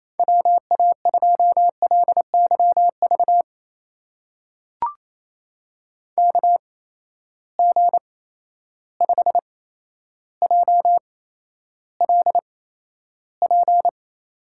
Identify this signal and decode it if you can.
Morse code, heard as a keyed tone, about 28 wpm, spaced as for 12 wpm, 700 Hz. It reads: WA2LYV KG5JLP
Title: USA Call Signs - 6 Characters - 8x Character Spacing - 28wpm